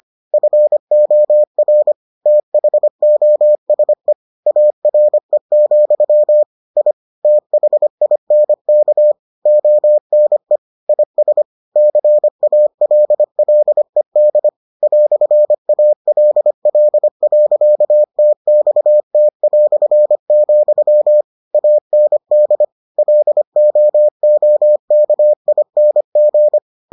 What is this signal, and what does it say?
Morse code, heard as a keyed tone, about 25 wpm, 600 Hz. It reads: FOR THOSE ARE, I THINK ONE IS CALLED "ALL.TXT", AND LOOKING